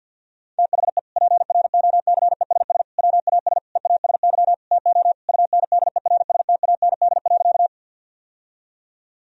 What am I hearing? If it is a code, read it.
Morse code, heard as a keyed tone, about 50 wpm, 700 Hz. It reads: THE PROCESS WAS EASY TO UNDERSTAND.
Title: Sentences from Top 1000 Words - Review Part 3 - 50wpm